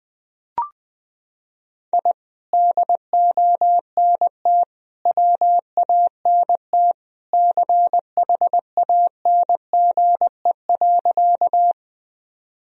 Morse code, heard as a keyed tone, about 20 wpm, 700 Hz. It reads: I DONT WANT CHANGE.
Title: Sentences from Top 1000 Words - Review Part 1 - 20wpm